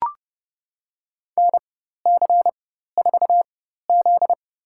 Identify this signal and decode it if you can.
Morse code, heard as a keyed tone, about 30 wpm, spaced as for 14 wpm, 700 Hz. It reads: NC4Z
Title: USA Call Signs - 4 Characters - 4x Character Spacing - 30wpm